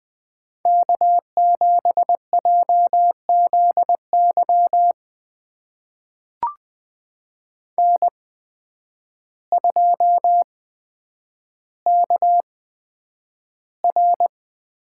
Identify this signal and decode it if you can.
Morse code, heard as a keyed tone, about 20 wpm, spaced as for 9 wpm, 700 Hz. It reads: K7JZY N2KR
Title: USA Call Signs - 5 Characters - 8x Character Spacing - 20wpm